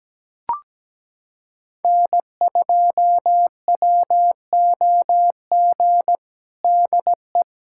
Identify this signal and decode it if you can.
Morse code, heard as a keyed tone, about 17 wpm, 700 Hz. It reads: N2WOG DE
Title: USA Callsign DE Callsign - 5 Characters - 17wpm